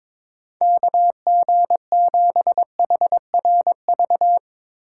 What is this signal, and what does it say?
Morse code, heard as a keyed tone, about 22 wpm, 700 Hz. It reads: KG7HRV
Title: USA Call Signs - 6 Characters - 2x Character Spacing - 22wpm